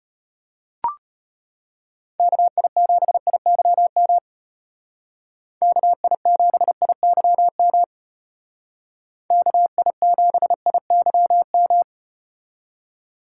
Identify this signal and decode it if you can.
Morse code, heard as a keyed tone, about 33 wpm, 700 Hz. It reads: KI7IYM KI7IYM KI7IYM